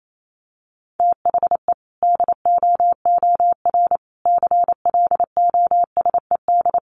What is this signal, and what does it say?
Morse code, heard as a keyed tone, about 28 wpm, 700 Hz. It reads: THE DOOR CLOSED
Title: Sentences from Top 1400 Words 28wpm